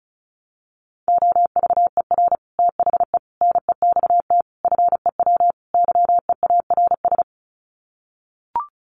Morse code, heard as a keyed tone, about 34 wpm, 700 Hz. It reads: OVER THE NEXT FEW YEARS
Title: Sets of 5 Words Encore 35wpm